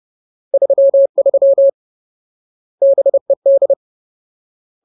Morse code, heard as a keyed tone, about 30 wpm, 550 Hz. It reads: 33 BED
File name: Ultimate Word List / 30WPM Ultimate